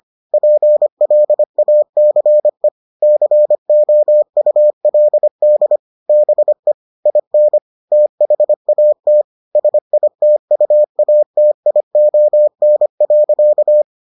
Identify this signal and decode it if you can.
Morse code, heard as a keyed tone, about 25 wpm, 600 Hz. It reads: PLACE COULD BE IN THAT SITUATION.